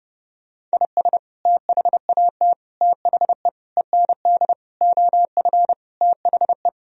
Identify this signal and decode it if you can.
Morse code, heard as a keyed tone, about 30 wpm, 700 Hz. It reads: IS THAT THE END OF THE
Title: Sentences from Top 1000 Words - Review Part 1 - 30wpm